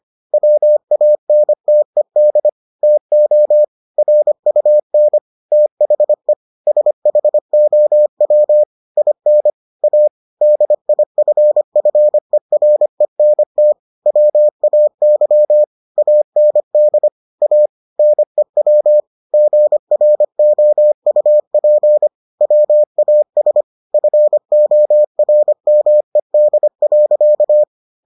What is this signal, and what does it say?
Morse code, heard as a keyed tone, about 25 wpm, 600 Hz. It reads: WANTED TO RUN THE SHOW IN A DIFFERENT WAY AND A NEW GROUP WAS FORMED.